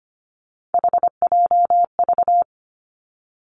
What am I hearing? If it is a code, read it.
Morse code, heard as a keyed tone, about 25 wpm, 700 Hz. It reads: HJV
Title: Three Challenging Letter-Number Combinations 25wpm